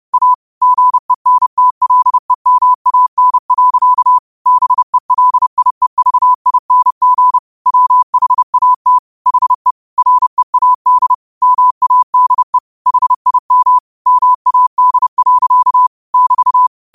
Morse code, heard as a keyed tone, about 30 wpm, 1000 Hz. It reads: A GENTLEMAN. BELIEVING WHAT HE READ MADE HIM MAD. <BT>
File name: qotdfiles / 30wpm